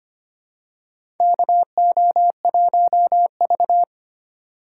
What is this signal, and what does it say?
Morse code, heard as a keyed tone, about 25 wpm, 700 Hz. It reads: KO1V
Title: USA Call Signs - 4 Characters - 2x Character Spacing - 25wpm